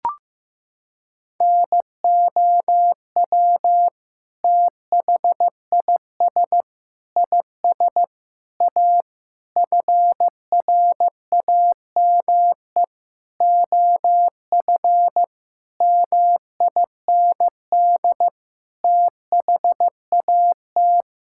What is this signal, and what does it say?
Morse code, heard as a keyed tone, about 16 wpm, 700 Hz. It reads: NOW THIS IS A FRAME OF MIND THAT
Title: Sentences from Top 1000 Words - Review Part 6 - 15wpm